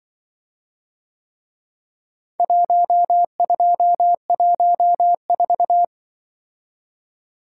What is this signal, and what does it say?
Morse code, heard as a keyed tone, about 24 wpm, 700 Hz. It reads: 1214
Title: Speed-Racing - Four Letter-Number Combinations - 20wpm